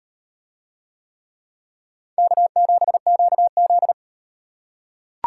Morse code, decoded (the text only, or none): K7QZ